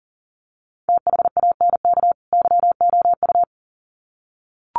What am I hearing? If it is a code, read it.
Morse code, heard as a keyed tone, about 40 wpm, 700 Hz. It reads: THANK YOU
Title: Sets of 2 Words 40wpm